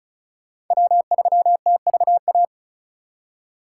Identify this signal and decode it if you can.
Morse code, heard as a keyed tone, about 39 wpm, 700 Hz. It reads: W3TVA
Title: Speed Racing - USA 1x3 Call Signs 35wpm